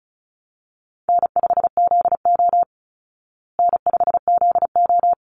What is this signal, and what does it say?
Morse code, heard as a keyed tone, about 34 wpm, 700 Hz. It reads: N5ZO N5ZO
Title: Call Signs Repeated 3x - 35wpm